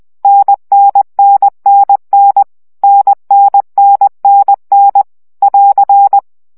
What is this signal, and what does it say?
Morse code, heard as a keyed tone, about 20 wpm, 800 Hz. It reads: NNNNN NNNNN <AR>
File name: CW gyakorlas n 17 baud